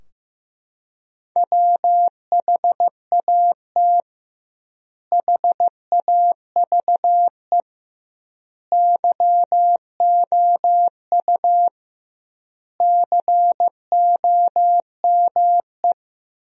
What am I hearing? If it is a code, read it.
Morse code, heard as a keyed tone, about 16 wpm, 700 Hz. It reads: WHAT HAVE YOU COME
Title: 2x Word-Spacing - Sentences from Top 100 Words - 15wpm